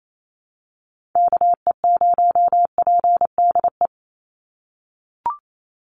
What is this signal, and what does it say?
Morse code, heard as a keyed tone, about 28 wpm, 700 Hz. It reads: KE0PDE
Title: USA Call Signs 28wpm